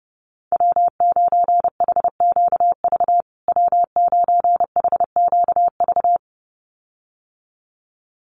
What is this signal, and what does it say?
Morse code, heard as a keyed tone, about 28 wpm, 700 Hz. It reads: W9HQV W9HQV
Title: USA Call Signs Repeated 2x 30wpm